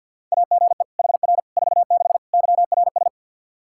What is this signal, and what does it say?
Morse code, heard as a keyed tone, about 51 wpm, 700 Hz. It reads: AGE HR 46 YRS
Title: Practice QSO - Exchange 3 - More Info - 50wpm